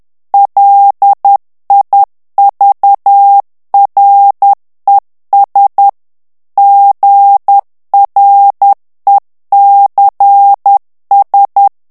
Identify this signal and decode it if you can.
Morse code, heard as a keyed tone, about 10 wpm, 800 Hz. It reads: LIVRES GRECS
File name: morse mp3 / FRANCAIS TEXTE CLAIR 2 VITESSE 10 MOTS MINUTES